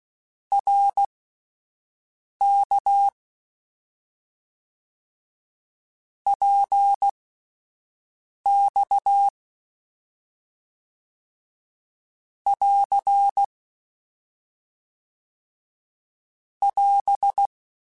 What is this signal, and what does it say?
Morse code, heard as a keyed tone, about 16 wpm, spaced as for 5 wpm, 800 Hz. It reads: RK PX <AR> <AS>
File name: Short Runs / Short Runs-mp3s / Balanced @10